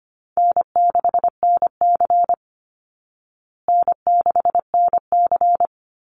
Morse code, heard as a keyed tone, about 25 wpm, 700 Hz. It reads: N6NC N6NC
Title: Call Signs Repeated 3x - 25wpm